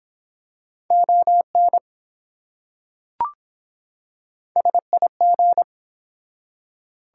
Morse code, heard as a keyed tone, about 24 wpm, 700 Hz. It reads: ON SIG